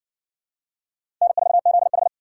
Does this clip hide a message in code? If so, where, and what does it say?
Yes: Morse code from 1.2 to 2.1 s, heard as a keyed tone, about 58 wpm, 700 Hz. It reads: N4ZH